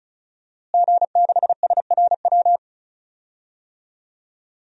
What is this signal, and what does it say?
Morse code, heard as a keyed tone, about 36 wpm, 700 Hz. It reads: G6SRW
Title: DXCC-100 Call Signs - 5 Characters - Repeated 2x - 35wpm